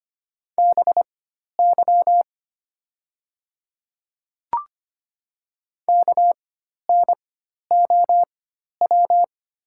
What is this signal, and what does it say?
Morse code, heard as a keyed tone, about 25 wpm, spaced as for 12 wpm, 700 Hz. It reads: BY KNOW
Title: 4x Character Spacing - Top 200 Words - 25wpm